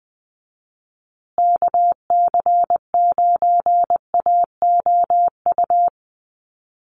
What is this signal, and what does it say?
Morse code, heard as a keyed tone, about 20 wpm, 700 Hz. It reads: KC9AOU
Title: USA Call Signs Repeated 3x 20wpm